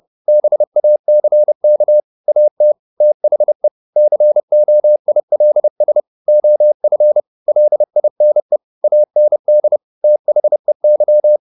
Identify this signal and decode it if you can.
Morse code, heard as a keyed tone, about 30 wpm, 600 Hz. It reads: BACK AT THE COILS OF LINE AND THEY